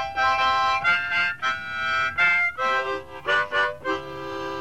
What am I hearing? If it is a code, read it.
Emergency Alert System (SAME) tones, heard as harsh buzzing bursts. It is heard from 0.8 to 3.7 s.